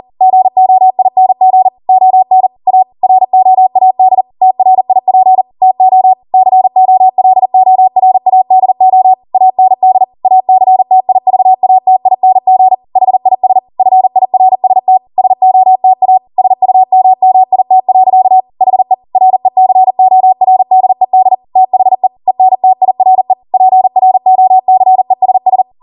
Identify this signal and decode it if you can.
Morse code, heard as a keyed tone, about 40 wpm, 750 Hz. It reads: GOING ON A ROAD TRIP TO COLORADO AND ACTIVATING HIS FIRST SOTA SUMMIT. HE RECORDED THE ENTIRE PROCESS